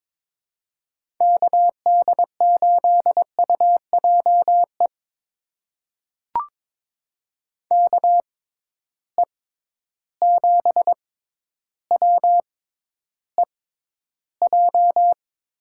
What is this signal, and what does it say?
Morse code, heard as a keyed tone, about 21 wpm, spaced as for 11 wpm, 700 Hz. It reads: KD8UJE KE7WEJ
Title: USA Call Signs - 6 Characters - 6x Character Spacing - 22wpm